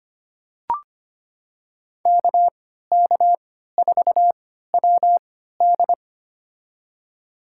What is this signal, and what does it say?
Morse code, heard as a keyed tone, about 25 wpm, spaced as for 14 wpm, 700 Hz. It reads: KK4WD
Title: USA Call Signs - 5 Characters - 3x Character Spacing - 25wpm